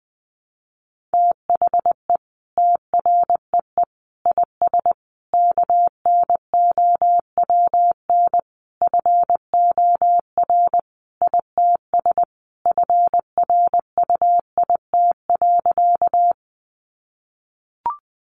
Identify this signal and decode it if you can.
Morse code, heard as a keyed tone, about 20 wpm, 700 Hz. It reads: THE TREE IS KNOWN FOR ITS FRUIT.
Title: Sentences from Top 1300 Words 20wpm